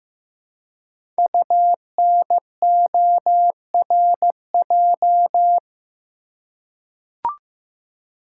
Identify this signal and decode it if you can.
Morse code, heard as a keyed tone, about 15 wpm, 700 Hz. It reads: UNORJ
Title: Code Groups - 5 Letters - 15wpm